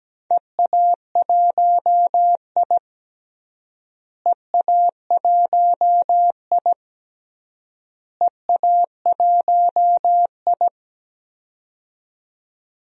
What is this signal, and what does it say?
Morse code, heard as a keyed tone, about 17 wpm, 700 Hz. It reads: EA1I EA1I EA1I
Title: DXCC-100 Call Signs - 4 Characters - Repeated 3x - 17wpm